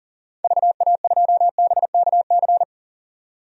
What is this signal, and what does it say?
Morse code, heard as a keyed tone, about 40 wpm, 700 Hz. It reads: VA2BKC
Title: DXCC-100 Call Signs - 6 Characters - 40wpm